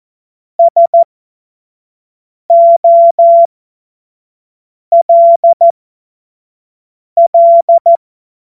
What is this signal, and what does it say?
Morse code, heard as a keyed tone, about 14 wpm, spaced as for 5 wpm, 680 Hz. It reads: SOLL
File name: Morsekurs / Witze / Witz 09 April 2024 25 F70